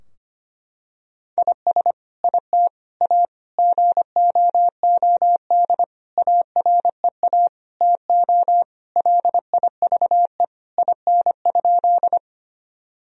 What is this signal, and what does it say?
Morse code, heard as a keyed tone, about 25 wpm, 700 Hz. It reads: IS IT A GOOD AREA TO LIVE IN?
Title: Sentences from Top 1000 Words - Review Part 2 - 25wpm